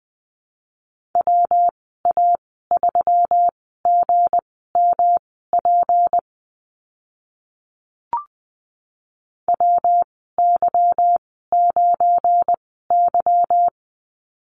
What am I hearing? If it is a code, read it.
Morse code, heard as a keyed tone, about 20 wpm, spaced as for 15 wpm, 700 Hz. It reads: WA3GMP WY9Y
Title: 186 Lesson 30g - Letter b - Call Signs 20wpm